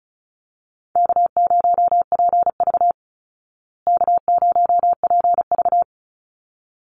Morse code, heard as a keyed tone, about 34 wpm, 700 Hz. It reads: K0PV K0PV
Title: Call Signs Repeated 3x - 35wpm